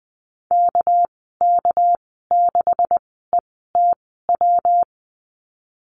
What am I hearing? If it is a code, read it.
Morse code, heard as a keyed tone, about 20 wpm, spaced as for 15 wpm, 700 Hz. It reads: KK6ETW
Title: 200 Lesson 32g - Letter k - Call Signs 20wpm